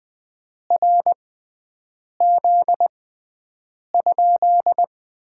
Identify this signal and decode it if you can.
Morse code, heard as a keyed tone, about 20 wpm, spaced as for 7 wpm, 700 Hz. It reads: RZ?
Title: Top 90 QSO Elements - 6x Character Spacing - 20wpm